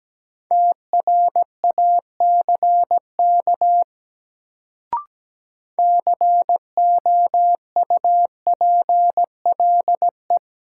Morse code, heard as a keyed tone, about 17 wpm, 700 Hz. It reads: TRACK COUPLE
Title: Sentences from Top 1100 Words - New Word Drill - 17wpm